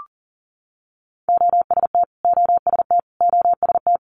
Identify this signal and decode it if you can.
Morse code, heard as a keyed tone, about 37 wpm, 700 Hz. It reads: OST OST OST
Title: Multi-letter Phonemes Repeated 3x 40wpm